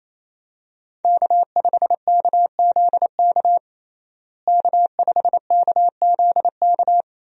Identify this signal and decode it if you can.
Morse code, heard as a keyed tone, about 28 wpm, 700 Hz. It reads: K5KZK K5KZK
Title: USA Call Signs - 5 Characters - Repeated 3x - 28wpm